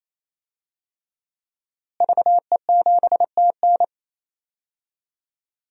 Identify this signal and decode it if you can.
Morse code, heard as a keyed tone, about 28 wpm, 690 Hz. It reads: VE7TN
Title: DXCC-100 Call Signs - 5 Characters - Repeated 3x - 28wpm